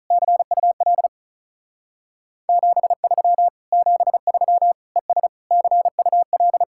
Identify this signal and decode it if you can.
Morse code, heard as a keyed tone, about 44 wpm, 700 Hz. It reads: CUL 73 73 ES CUL